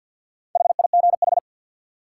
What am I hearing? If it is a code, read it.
Morse code, heard as a keyed tone, about 53 wpm, 700 Hz. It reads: HIGH
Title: Sets of 2 Words - New Word Drill - 50wpm